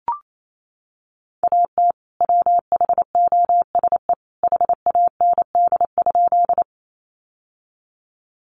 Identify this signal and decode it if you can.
Morse code, heard as a keyed tone, about 28 wpm, 700 Hz. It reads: AT WHOSE HAND?
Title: Sentences from Top 1200 Words 28wpm